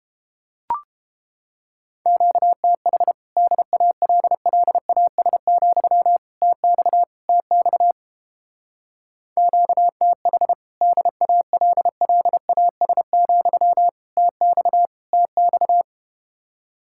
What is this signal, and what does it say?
Morse code, heard as a keyed tone, about 32 wpm, 700 Hz. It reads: QTH DALLAS, TX TX QTH DALLAS, TX TX